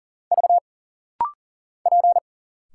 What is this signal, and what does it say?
Morse code, heard as a keyed tone, about 39 wpm, 700 Hz. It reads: V P
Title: Single Letters - Rapid-Fire 40wpm